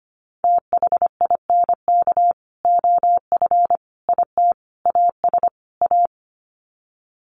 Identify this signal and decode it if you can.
Morse code, heard as a keyed tone, about 25 wpm, 700 Hz. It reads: THINK OF IT AS A
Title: Sets of 5 Words Encore2 25wpm